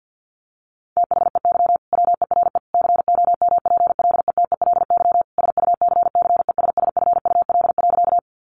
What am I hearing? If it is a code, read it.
Morse code, heard as a keyed tone, about 50 wpm, 700 Hz. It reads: THEY WERE COMPLETELY SUCCESSFUL.
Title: Sentences from Top 1100 Words 50wpm